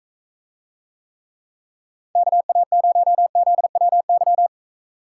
Morse code, heard as a keyed tone, about 44 wpm, 700 Hz. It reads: KA0ZWY